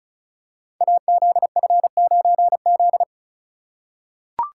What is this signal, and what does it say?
Morse code, heard as a keyed tone, about 36 wpm, 700 Hz. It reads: AZF9Z
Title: Code Groups - 5 Letter:Numbers - 35wpm